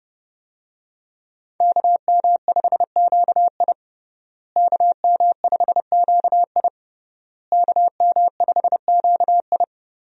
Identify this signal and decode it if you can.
Morse code, heard as a keyed tone, about 30 wpm, 700 Hz. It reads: KM5QI KM5QI KM5QI
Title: USA Call Signs - 5 Characters - Repeated 3x - 30wpm